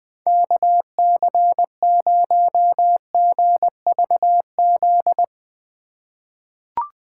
Morse code, heard as a keyed tone, about 20 wpm, 700 Hz. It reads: KC0GVZ
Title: USA Call Signs - 6 Characters - 2x Character Spacing - 20wpm